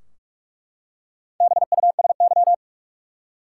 Morse code, heard as a keyed tone, about 43 wpm, 700 Hz. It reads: BUSY
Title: Four Letter Words - 45wpm